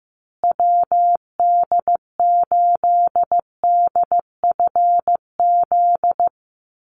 Morse code, heard as a keyed tone, about 15 wpm, 700 Hz. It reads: WD8DFZ
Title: USA Call Signs Repeated 2x 15wpm